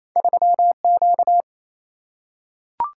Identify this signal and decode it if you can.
Morse code, heard as a keyed tone, about 28 wpm, 700 Hz. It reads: 3Q
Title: Single-Letter Single-Number 28wpm